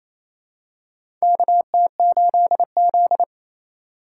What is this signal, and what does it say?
Morse code, heard as a keyed tone, about 28 wpm, 700 Hz. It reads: KT8Z
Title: USA Call Signs - 4 Characters - 2x Character Spacing - 28wpm